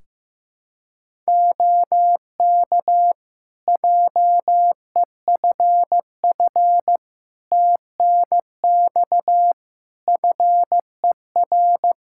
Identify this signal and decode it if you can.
Morse code, heard as a keyed tone, about 15 wpm, 700 Hz. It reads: OK JEFF TNX FER
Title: Practice QSO - Exchange 4 - Ending - 15wpm